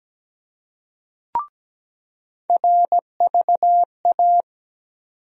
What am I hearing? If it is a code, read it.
Morse code, heard as a keyed tone, about 17 wpm, 700 Hz. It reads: RVA
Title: USA Call Sign Postfixes - 3 Letters 17wpm